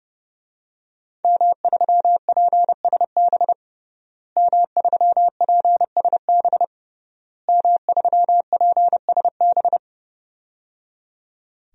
Morse code, heard as a keyed tone, about 30 wpm, 700 Hz. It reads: M3PSB M3PSB M3PSB
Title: DXCC-100 Call Signs - 5 Characters - Repeated 3x - 30wpm